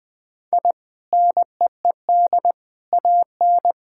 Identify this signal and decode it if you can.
Morse code, heard as a keyed tone, about 21 wpm, 700 Hz. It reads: I NEED AN
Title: Sentences from Top 1000 Words - Review Part 5 - 20wpm